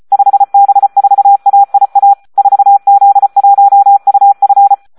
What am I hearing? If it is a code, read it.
Morse code, heard as a keyed tone, about 34 wpm, 800 Hz. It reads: 5B4AIA 4Z1UF
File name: Audio practica CW / SD 1 34